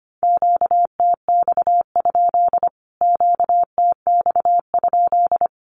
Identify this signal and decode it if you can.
Morse code, heard as a keyed tone, about 25 wpm, 700 Hz. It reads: QTX? QTX?
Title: Q-Codes Repeated 3x 25wpm